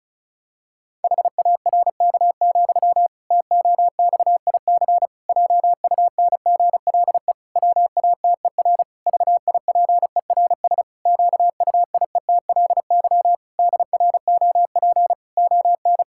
Morse code, heard as a keyed tone, about 36 wpm, 700 Hz. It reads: HARK, TOXIC JUNGLE WATER VIPERS QUIETLY DROP ON
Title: Pangram Challenge v2 - 35wpm